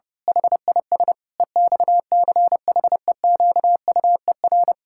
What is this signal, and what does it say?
Morse code, heard as a keyed tone, about 30 wpm, 700 Hz. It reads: HIS EXCHEQUER
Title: Pangram Challenge v2 - 30wpm